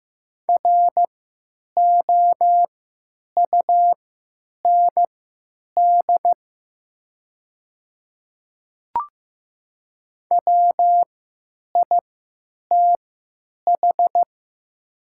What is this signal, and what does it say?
Morse code, heard as a keyed tone, about 15 wpm, spaced as for 9 wpm, 700 Hz. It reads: ROUND WITH
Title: 3x Character Spacing - Top 200 Words - 15wpm